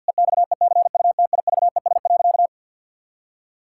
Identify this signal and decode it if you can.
Morse code, heard as a keyed tone, about 53 wpm, 700 Hz. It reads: EXECUTIVES.
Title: Pangram Challenge v2 - 50wpm